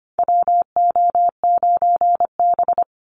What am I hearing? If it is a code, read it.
Morse code, heard as a keyed tone, about 25 wpm, 700 Hz. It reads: WO9B
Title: Call Signs Repeated 3x - 25wpm